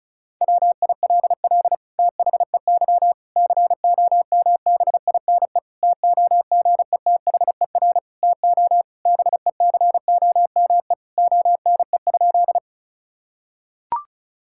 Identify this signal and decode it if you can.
Morse code, heard as a keyed tone, about 36 wpm, 700 Hz. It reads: WILL THEY COMBINE TOGETHER TO BECOME ONE?
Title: Sentences from Top 1600 Words 35wpm